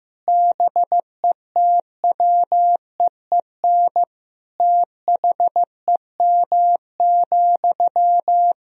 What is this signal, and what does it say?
Morse code, heard as a keyed tone, about 16 wpm, 700 Hz. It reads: BETWEEN THEM,
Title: Sentences from Top 1000 Words - Review Part 1 - 15wpm